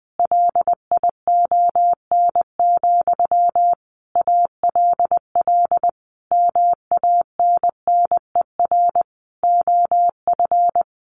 Morse code, heard as a keyed tone, about 19 wpm, 700 Hz. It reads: LION, ALL MANNER OF
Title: A Sound Of Thunder 20wpm